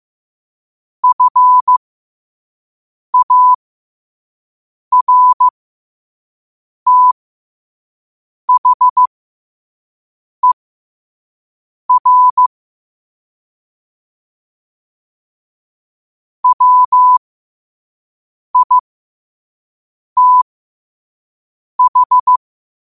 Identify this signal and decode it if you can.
Morse code, heard as a keyed tone, about 15 wpm, spaced as for 5 wpm, 1000 Hz. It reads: FARTHER WITH
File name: qotdfiles / 5wpm